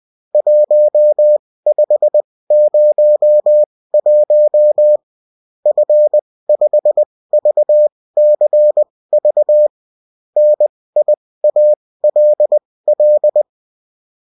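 Morse code, heard as a keyed tone, about 20 wpm, spaced as for 16 wpm, 600 Hz. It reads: 1501 F5VCV NIALL
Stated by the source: Morse code bulletin of the UFT.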